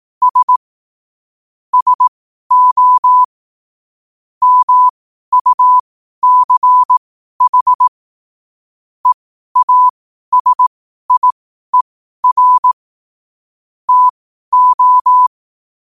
Morse code, heard as a keyed tone, about 18 wpm, spaced as for 12 wpm, 1000 Hz. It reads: S SO MUCH EASIER TO
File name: qotdfiles / 12wpm